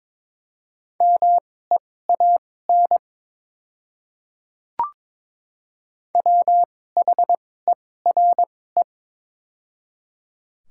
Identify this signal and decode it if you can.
Morse code, heard as a keyed tone, about 22 wpm, spaced as for 16 wpm, 700 Hz. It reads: MEAN WHERE
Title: 2x Character Spacing - Top 200 Words - 22wpm